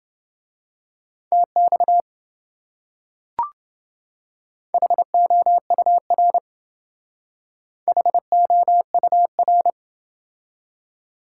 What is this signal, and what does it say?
Morse code, heard as a keyed tone, about 28 wpm, 700 Hz. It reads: TX HOUR HOUR